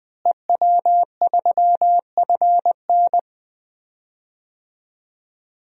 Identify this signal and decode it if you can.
Morse code, heard as a keyed tone, about 20 wpm, 700 Hz. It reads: EW3FN
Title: DXCC-100 Call Signs - 5 Characters - Repeated 2x - 20wpm